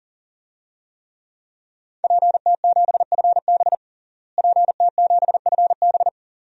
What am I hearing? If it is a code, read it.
Morse code, heard as a keyed tone, about 40 wpm, 700 Hz. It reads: PT7FB PT7FB
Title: DXCC-100 Call Signs - 5 Characters - Repeated 2x - 40wpm